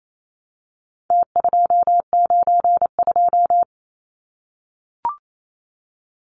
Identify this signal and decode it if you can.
Morse code, heard as a keyed tone, about 28 wpm, 700 Hz. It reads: T292
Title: Numbers - Contest Style - 4-Digit 28wpm